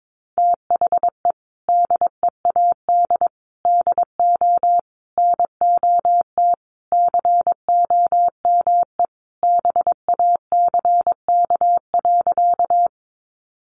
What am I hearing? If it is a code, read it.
Morse code, heard as a keyed tone, about 22 wpm, 700 Hz. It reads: THE DEAD DO NOT COME BACK.
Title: Sentences from Top 1500 Words 22wpm